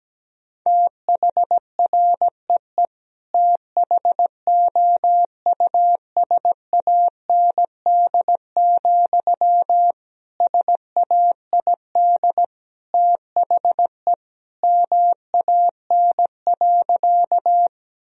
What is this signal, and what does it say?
Morse code, heard as a keyed tone, about 17 wpm, 700 Hz. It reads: THREE THOUSAND, SAID THE MAN.
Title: Sentences from Top 1000 Words - Review Part 3 - 17wpm